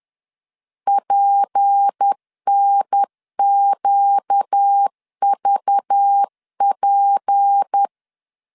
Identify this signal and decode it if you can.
Morse code, heard as a keyed tone, about 10 wpm, 800 Hz. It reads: PNQVP
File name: LEZIONI / 6 PKX0 / 6 P 10wpm 2432